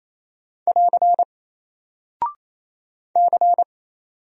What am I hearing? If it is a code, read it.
Morse code, heard as a keyed tone, about 28 wpm, 700 Hz. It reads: <AR> C
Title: Single Character (Letter:Number:Punctuation:Prosign) 28wpm